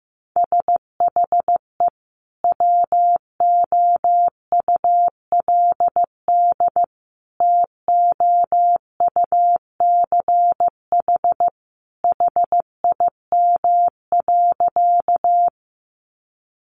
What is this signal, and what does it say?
Morse code, heard as a keyed tone, about 15 wpm, 700 Hz. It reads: SHE WOULD TOUCH HIM.
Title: Sentences from Top 1200 Words 15wpm